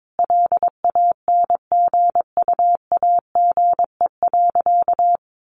Morse code, heard as a keyed tone, about 22 wpm, 700 Hz. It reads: LANGUAGE.
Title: Sentences from Top 1300 Words 22wpm